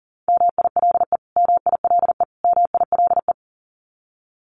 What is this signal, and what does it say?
Morse code, heard as a keyed tone, about 40 wpm, 700 Hz. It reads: MILE MILE MILE
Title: Top 500 Words Repeated 40wpm